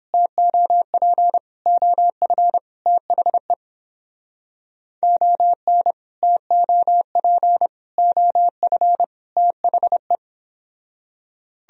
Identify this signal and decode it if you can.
Morse code, heard as a keyed tone, about 31 wpm, 700 Hz. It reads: TOP OF THE ON TOP OF THE